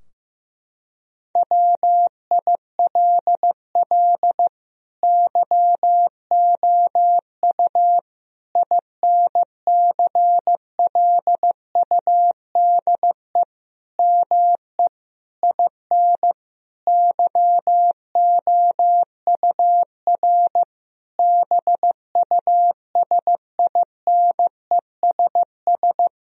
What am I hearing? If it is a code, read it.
Morse code, heard as a keyed tone, about 15 wpm, 700 Hz. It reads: WILL YOU INCLUDE ME IN YOUR BUSINESS
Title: Sentences from Top 1000 Words - Review Part 3 - 15wpm